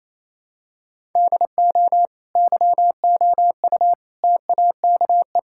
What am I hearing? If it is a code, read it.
Morse code, heard as a keyed tone, about 27 wpm, 700 Hz. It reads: DO YOU TAKE
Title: Sentences from Top 1000 Words - Review Part 6 - 28wpm